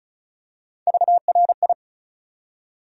Morse code, heard as a keyed tone, about 34 wpm, 700 Hz. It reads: VRI
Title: USA Call Sign Postfixes - 3 Letters 35wpm